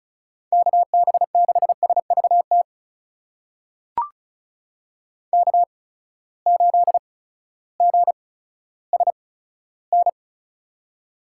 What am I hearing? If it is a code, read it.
Morse code, heard as a keyed tone, about 36 wpm, spaced as for 16 wpm, 700 Hz. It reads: KB6SVT K8GSN
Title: USA Call Signs - 8x Character Spacing - 35wpm